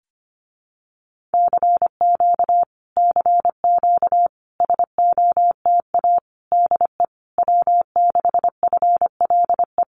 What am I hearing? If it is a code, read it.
Morse code, heard as a keyed tone, about 25 wpm, 700 Hz. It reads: CQ CQ SOTA DE W6FLE
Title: Common QSO Phrases Encore 25wpm